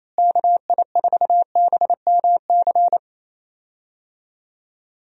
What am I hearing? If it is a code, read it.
Morse code, heard as a keyed tone, about 28 wpm, 700 Hz. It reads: KI4BMC
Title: USA Call Signs - 6 Characters - Repeated 3x - 28wpm